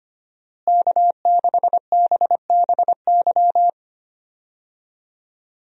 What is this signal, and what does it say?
Morse code, heard as a keyed tone, about 25 wpm, 700 Hz. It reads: K6BBY
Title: USA Call Signs - 5 Characters - Repeated 2x - 25wpm